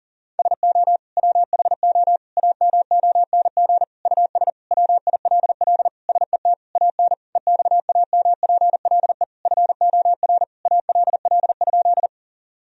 Morse code, heard as a keyed tone, about 40 wpm, 700 Hz. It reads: SO WHO AMONG US WILL SET AN EXAMPLE FOR ALL?
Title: Sentences from Top 1000 Words - Review Part 3 - 40wpm